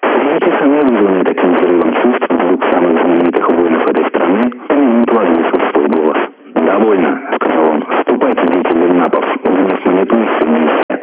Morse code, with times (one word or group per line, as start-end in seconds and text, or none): none